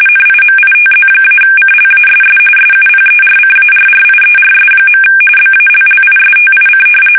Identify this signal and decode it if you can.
RTTY, about 45.45 baud. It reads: NIMITZ-CLASS SUPERCARRIERS. BALLISTIC
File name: fsk850 / usnavy5